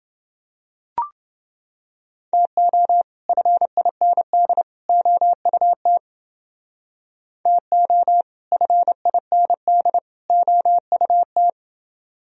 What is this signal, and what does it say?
Morse code, heard as a keyed tone, about 29 wpm, 700 Hz. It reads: TO FIND OUT TO FIND OUT